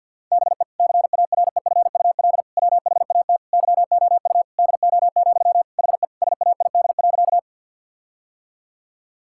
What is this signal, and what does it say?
Morse code, heard as a keyed tone, about 50 wpm, 700 Hz. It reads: BE CAREFUL WHAT YOU DO, HE SAID.
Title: Sentences from Top 1600 Words 50wpm